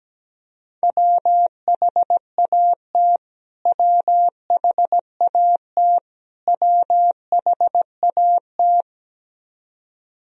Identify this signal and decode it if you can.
Morse code, heard as a keyed tone, about 17 wpm, 700 Hz. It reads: WHAT WHAT WHAT
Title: Top 100 Words Repeated 17wpm